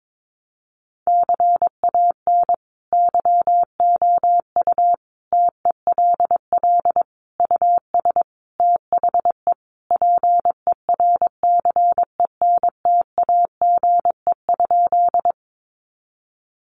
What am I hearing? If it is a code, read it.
Morse code, heard as a keyed tone, about 22 wpm, 700 Hz. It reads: CAN YOU TELL US THE PERCENTAGE?
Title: Sentences from Top 1300 Words 22wpm